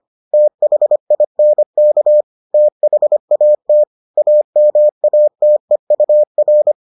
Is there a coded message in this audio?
Yes: Morse code from 0.3 to 6.7 s, heard as a keyed tone, about 25 wpm, 600 Hz. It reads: THINK THAT AMATEUR